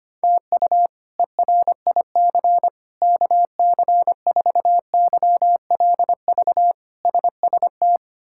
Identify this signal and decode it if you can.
Morse code, heard as a keyed tone, about 25 wpm, 700 Hz. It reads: TU ERIC KC4YLV SST
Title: K1USN SST QSO Practice - 25wpm